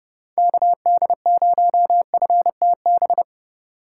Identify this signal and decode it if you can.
Morse code, heard as a keyed tone, about 37 wpm, 700 Hz. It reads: KD0FTB
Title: Speed Racing - USA 2x3 Call Signs 30wpm